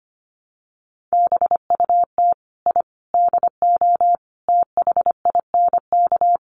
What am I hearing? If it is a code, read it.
Morse code, heard as a keyed tone, about 25 wpm, 700 Hz. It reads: BUT I DO THINK
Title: Sets of 5 Words Encore2 25wpm